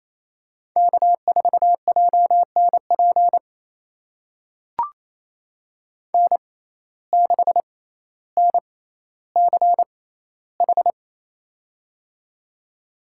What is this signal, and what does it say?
Morse code, heard as a keyed tone, about 28 wpm, spaced as for 14 wpm, 700 Hz. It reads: K4JNP N6NCH
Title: USA Call Signs - 6x Character Spacing - 28wpm